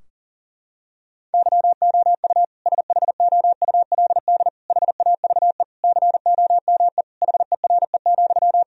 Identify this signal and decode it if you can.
Morse code, heard as a keyed tone, about 40 wpm, 700 Hz. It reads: YOU SHOULD HAVE COME HERE,
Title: Sentences from Top 1000 Words - Review Part 4 - 40wpm